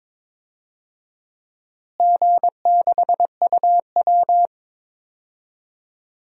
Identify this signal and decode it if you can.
Morse code, heard as a keyed tone, about 21 wpm, 700 Hz. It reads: G6UW
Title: DXCC-100 Call Signs - 4 Characters - Repeated 2x - 22wpm